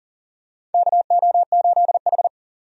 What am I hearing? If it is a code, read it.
Morse code, heard as a keyed tone, about 40 wpm, 690 Hz. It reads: KO8H
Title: USA Call Signs - 4 Characters - 40wpm